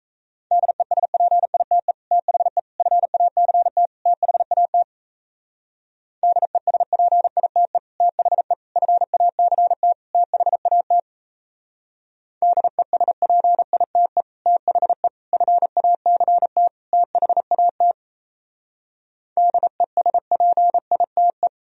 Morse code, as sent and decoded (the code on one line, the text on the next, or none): -.. ... .--. .. - . / - .... / ..-. .- -.-. - / - .... .- - / -.. ... .--. .. - . / - .... / ..-. .- -.-. - / - .... .- - / -.. ... .--. .. - . / - .... / ..-. .- -.-. - / - .... .- - / -.. ... .--. .. - .
DESPITE THE FACT THAT DESPITE THE FACT THAT DESPITE THE FACT THAT DESPITE